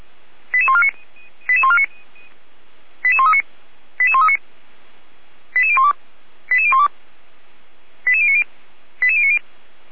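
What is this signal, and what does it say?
Five-tone selective call (ZVEI), heard as quick runs of five tones, tones of about 70 ms. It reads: 88149 88139 88139 88012 88012 88090 88090
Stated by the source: ZVEI five-tone selective call received on SDR, from a server of the Feuerwehr Orlamünde fire brigade.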